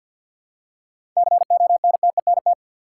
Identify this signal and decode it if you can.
Morse code, heard as a keyed tone, about 52 wpm, 700 Hz. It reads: CONTENT
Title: Seven Letter Words - 50wpm